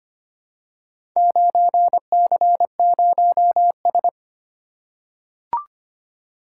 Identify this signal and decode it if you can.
Morse code, heard as a keyed tone, about 25 wpm, 700 Hz. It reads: 9C0S
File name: practice / 25wpm / Four Letter-Number Combinations - 25wpm